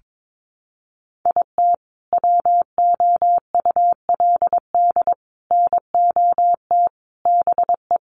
Morse code, heard as a keyed tone, about 22 wpm, 700 Hz. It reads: IT WOULD NOT BE
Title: Sentences from Top 1200 Words 22wpm